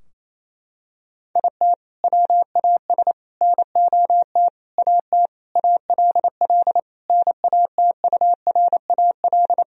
Morse code, heard as a keyed tone, about 28 wpm, 700 Hz. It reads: IT WAS NOT AT ALL NATURAL
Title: Sentences from Top 1000 Words - Review Part 4 - 28wpm